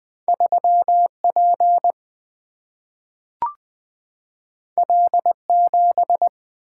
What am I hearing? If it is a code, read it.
Morse code, heard as a keyed tone, about 20 wpm, 700 Hz. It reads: 3 P L 7
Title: Single-Letter Single-Number 20wpm